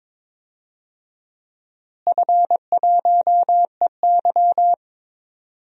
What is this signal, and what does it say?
Morse code, heard as a keyed tone, about 22 wpm, 700 Hz. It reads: F1EY
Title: DXCC-100 Call Signs - 4 Characters - Repeated 3x - 22wpm